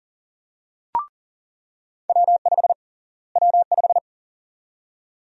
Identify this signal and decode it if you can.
Morse code, heard as a keyed tone, about 40 wpm, 700 Hz. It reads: W5 W5
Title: USA Call Sign Prefixes - Repeated 2x - 40wpm